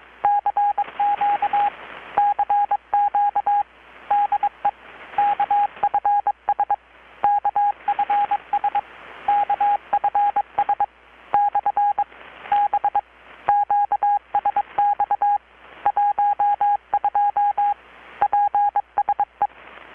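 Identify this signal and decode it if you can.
Morse code, heard as a keyed tone, about 22 wpm, 820 Hz, with radio noise. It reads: CQ CQ DE KFS KFS KFS/B QSX 12 PSE
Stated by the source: Morse from KPH's Night of Nights XX maritime radiotelegraphy event, received in Chiapas, Mexico about 3,800 km away on a Kenwood R-600 with a grounded 10 m long-wire antenna.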